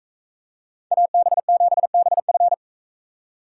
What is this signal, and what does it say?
Morse code, heard as a keyed tone, about 41 wpm, 700 Hz. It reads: AB7BF